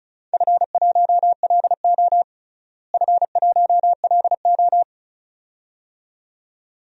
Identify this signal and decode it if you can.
Morse code, heard as a keyed tone, about 36 wpm, 700 Hz. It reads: F1LO F1LO
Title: DXCC-100 Call Signs - 4 Characters - Repeated 2x - 35wpm